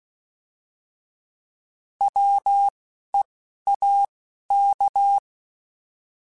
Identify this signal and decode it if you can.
Morse code, heard as a keyed tone, about 16 wpm, 800 Hz. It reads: W E A K
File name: Alphabetical / Alphabetical mp3s / Ws @10